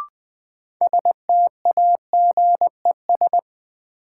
Morse code, heard as a keyed tone, about 20 wpm, 700 Hz. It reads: STAGES
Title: Sets of 5 Words Encore - New Word Drill - 20wpm